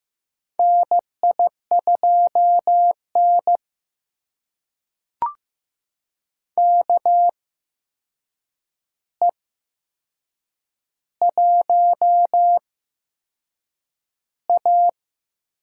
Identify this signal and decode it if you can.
Morse code, heard as a keyed tone, about 15 wpm, spaced as for 6 wpm, 700 Hz. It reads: NI2N KE1A
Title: USA Call Signs - 4 Characters - 8x Character Spacing - 15wpm